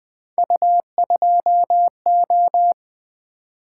Morse code, heard as a keyed tone, about 20 wpm, 700 Hz. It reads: U2O
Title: Three Letter-Number Combinations - 20wpm